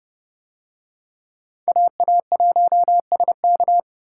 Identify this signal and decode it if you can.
Morse code, heard as a keyed tone, about 29 wpm, 700 Hz. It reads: AA1SK